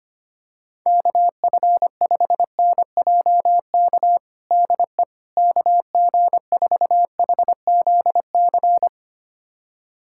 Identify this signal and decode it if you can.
Morse code, heard as a keyed tone, about 25 wpm, 700 Hz. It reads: KF5NJK DE KG4HZC
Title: USA Callsign DE Callsign - 6 Characters - 25wpm